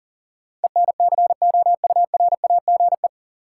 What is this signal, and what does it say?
Morse code, heard as a keyed tone, about 40 wpm, 700 Hz. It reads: ENCOURAGE
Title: Sentences from Top 1400 Words - New Word Drill Alt - 40wpm